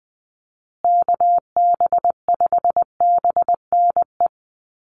Morse code, heard as a keyed tone, about 20 wpm, 700 Hz. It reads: KB5BNE
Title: USA Call Signs Repeated 3x 20wpm